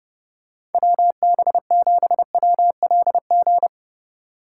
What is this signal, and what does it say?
Morse code, heard as a keyed tone, about 30 wpm, 700 Hz. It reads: WB7WLG